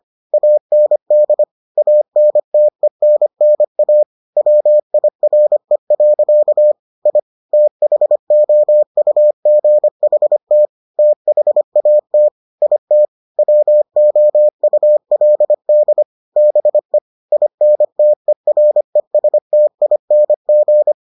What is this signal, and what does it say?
Morse code, heard as a keyed tone, about 25 wpm, 600 Hz. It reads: AND ANTENNA WIRE. I THOUGHT THAT IT WOULD BE INTERESTING